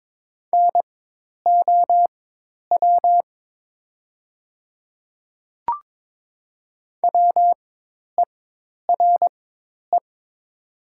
Morse code, heard as a keyed tone, about 22 wpm, spaced as for 10 wpm, 700 Hz. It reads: NOW WERE
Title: 4x Character Spacing - Top 100 Words - 22wpm